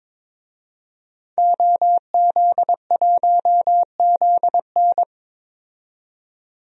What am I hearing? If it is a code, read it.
Morse code, heard as a keyed tone, about 22 wpm, 700 Hz. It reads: OZ1ZN
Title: DXCC-100 Call Signs - 5 Characters - 22wpm